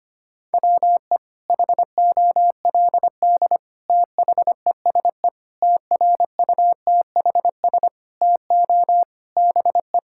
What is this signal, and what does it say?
Morse code, heard as a keyed tone, about 25 wpm, 700 Hz. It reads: WE HOLD THESE TRUTHS TO BE
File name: practice / 25wpm / Sentences with 9-Word Phrases - 25wpm